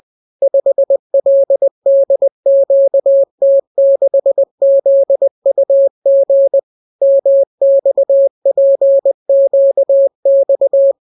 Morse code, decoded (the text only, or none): 5LDQT6ZUG MXPQX